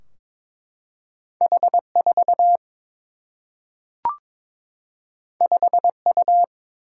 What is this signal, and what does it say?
Morse code, heard as a keyed tone, about 22 wpm, 700 Hz. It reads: H4 5U
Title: Single-Letter Single-Number 22wpm